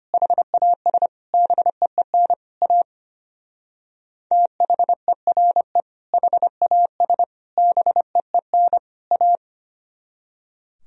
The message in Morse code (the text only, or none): HAS BEEN A THERE HAS BEEN A